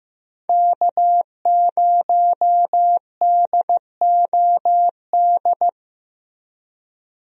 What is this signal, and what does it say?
Morse code, heard as a keyed tone, about 15 wpm, 700 Hz. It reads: K0DOD
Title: USA Call Signs - 5 Characters - Repeated 3x - 15wpm